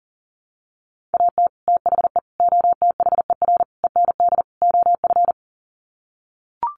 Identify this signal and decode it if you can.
Morse code, heard as a keyed tone, about 40 wpm, 700 Hz. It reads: AT THE OTHER END OF
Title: Sets of 5 Words 40wpm